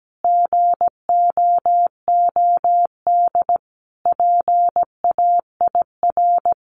Morse code, heard as a keyed tone, about 17 wpm, 700 Hz. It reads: GOOD PAIR
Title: Sentences from Top 1200 Words 17wpm